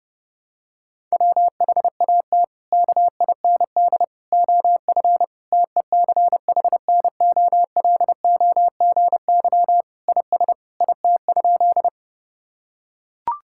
Morse code, heard as a keyed tone, about 30 wpm, 700 Hz. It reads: WHAT KIND OF TECHNOLOGY IS IT?
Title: Sentences from Top 1000 Words - Review Part 5 - 30wpm